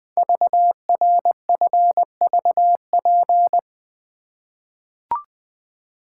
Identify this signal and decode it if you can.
Morse code, heard as a keyed tone, about 20 wpm, 700 Hz. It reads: VRFVP
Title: Code Groups - 5 Letters - 20wpm